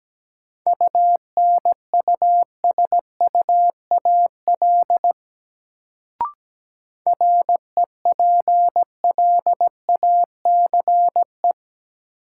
Morse code, heard as a keyed tone, about 17 wpm, 700 Hz. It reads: UNUSUAL REPLACE
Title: Sentences from Top 1400 Words - New Word Drill Alt - 17wpm